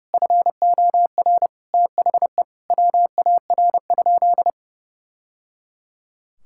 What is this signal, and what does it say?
Morse code, heard as a keyed tone, about 30 wpm, 700 Hz. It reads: FOR THE WAR?
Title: Sentences from Top 1000 Words - Review Part 5 - 30wpm